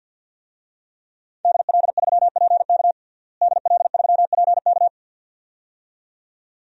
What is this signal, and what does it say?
Morse code, heard as a keyed tone, about 50 wpm, 700 Hz. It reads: DL3PK DL3PK
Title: DXCC-100 Call Signs - 5 Characters - Repeated 2x - 50wpm